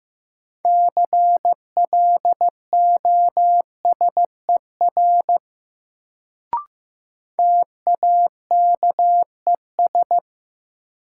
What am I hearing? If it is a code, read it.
Morse code, heard as a keyed tone, about 15 wpm, 700 Hz. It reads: CLOSER TAKES
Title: Sets of 5 Words Encore - New Word Drill - 15wpm